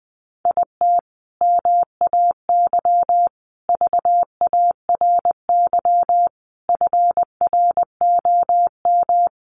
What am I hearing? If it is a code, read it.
Morse code, heard as a keyed tone, about 20 wpm, 700 Hz. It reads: IT MAY VARY FROM
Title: Sentences from Top 1200 Words 20wpm